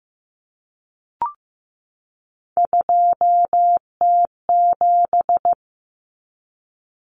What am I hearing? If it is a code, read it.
Morse code, heard as a keyed tone, about 15 wpm, 700 Hz. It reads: 2T7
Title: Common Cut Numbers - 3-Digit 15wpm